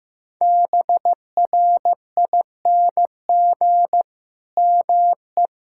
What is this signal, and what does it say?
Morse code, heard as a keyed tone, about 15 wpm, 700 Hz. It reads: BRING ME
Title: Sentences from Top 1600 Words 15wpm